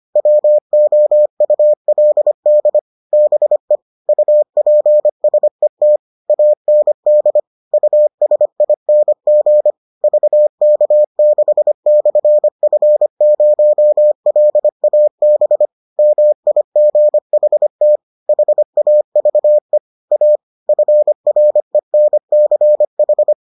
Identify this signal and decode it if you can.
Morse code, heard as a keyed tone, about 25 wpm, 600 Hz. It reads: WOULD BE UPSET AND USING VK6/F0LAB MIGHT HAVE A FRENCH